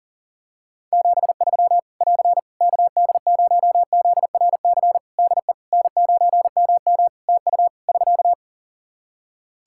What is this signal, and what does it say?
Morse code, heard as a keyed tone, about 37 wpm, 700 Hz. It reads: 73 <AR> KD0ZRC DE N9MM TU <SK>
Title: Practice QSO - Exchange 4 - Ending - 40wpm